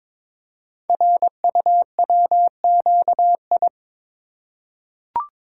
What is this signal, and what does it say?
Morse code, heard as a keyed tone, about 23 wpm, 700 Hz. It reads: RUWQI
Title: Code Groups - 5 Letters - 22wpm